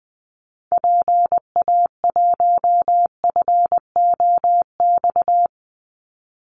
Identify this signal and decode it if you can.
Morse code, heard as a keyed tone, about 20 wpm, 700 Hz. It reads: PA1FOX
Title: Call Signs Repeated 2x - 20wpm